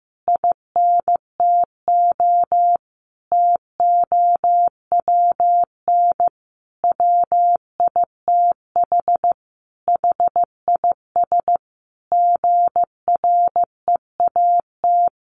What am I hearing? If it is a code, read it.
Morse code, heard as a keyed tone, about 15 wpm, 700 Hz. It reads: INTO TOWN WITH HIS GREAT
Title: Sentences from Top 1300 Words 15wpm